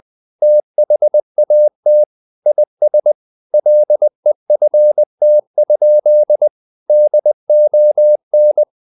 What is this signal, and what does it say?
Morse code, heard as a keyed tone, about 20 wpm, 600 Hz. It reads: THAT IS LEFT? DON
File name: the old man and the sea / the old man and the sea part 1 20wpm